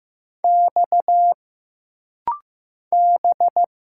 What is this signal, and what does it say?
Morse code, heard as a keyed tone, about 15 wpm, 700 Hz. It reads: X B
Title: Single Letter-Number Alt - Rapid-Fire 15wpm